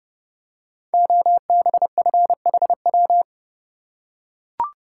Morse code, heard as a keyed tone, about 30 wpm, 700 Hz. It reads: OBFHW
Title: Code Groups - 5 Letters - 30wpm